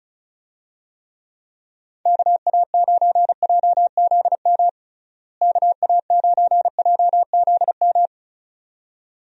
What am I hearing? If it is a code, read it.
Morse code, heard as a keyed tone, about 36 wpm, 700 Hz. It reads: KA9JZM KA9JZM
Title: USA Call Signs - 6 Characters - Repeated 2x - 35wpm